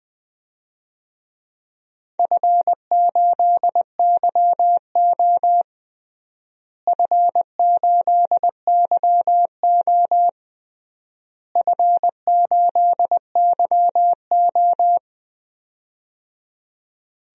Morse code, heard as a keyed tone, about 20 wpm, 700 Hz. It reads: F8YO F8YO F8YO
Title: DXCC-100 Call Signs - 4 Characters - Repeated 3x - 20wpm